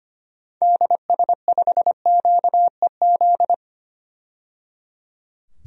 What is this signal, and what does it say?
Morse code, heard as a keyed tone, about 25 wpm, 700 Hz. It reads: DS5QEZ
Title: DXCC-100 Call Signs - 6 Characters - 25wpm